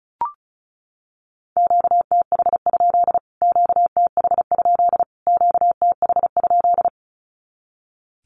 Morse code, heard as a keyed tone, about 34 wpm, 700 Hz. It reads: QTH? QTH? QTH?
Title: Q-Codes Repeated 3x 35wpm